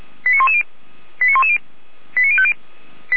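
Five-tone selective call (ZVEI), heard as quick runs of five tones, tones of about 70 ms. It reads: 89110 89110 89050
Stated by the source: ZVEI five-tone selective call received via SDR.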